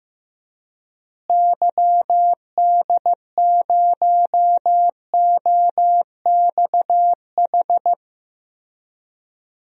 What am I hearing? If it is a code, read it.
Morse code, heard as a keyed tone, about 15 wpm, 700 Hz. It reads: YD0OXH
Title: DXCC-100 Call Signs - 6 Characters - Repeated 2x - 15wpm